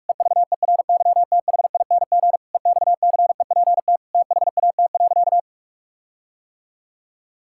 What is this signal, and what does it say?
Morse code, heard as a keyed tone, about 46 wpm, 700 Hz. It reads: EVERYTHING EXCEPT THAT.
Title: Sentences from Top 1600 Words 45wpm